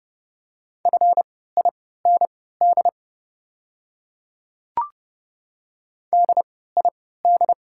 Morse code, heard as a keyed tone, about 30 wpm, spaced as for 17 wpm, 690 Hz. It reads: FIND DID
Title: 3x Character Spacing - Top 100 Words - 30wpm